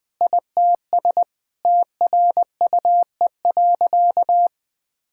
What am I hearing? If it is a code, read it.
Morse code, heard as a keyed tone, about 20 wpm, 700 Hz. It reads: ITS TRUE.
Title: Sentences with 7-Word Phrases - 20wpm